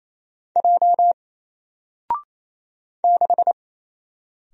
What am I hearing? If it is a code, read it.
Morse code, heard as a keyed tone, about 27 wpm, 700 Hz. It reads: J 6
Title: Single Letter-Number Alt 28wpm